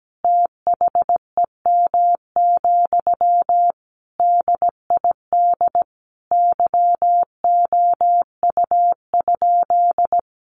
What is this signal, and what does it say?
Morse code, heard as a keyed tone, about 17 wpm, 700 Hz. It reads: THEM, DID YOU?
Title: Sentences from Top 1500 Words 17wpm